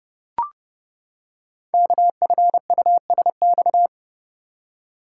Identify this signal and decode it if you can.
Morse code, heard as a keyed tone, about 30 wpm, 700 Hz. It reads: KFUSX
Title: Code Groups - 5 Letters - 30wpm